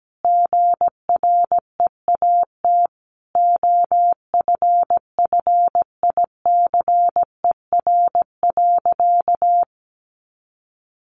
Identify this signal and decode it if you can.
Morse code, heard as a keyed tone, about 17 wpm, 700 Hz. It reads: GREAT OFFICER.
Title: Sentences from Top 1200 Words 17wpm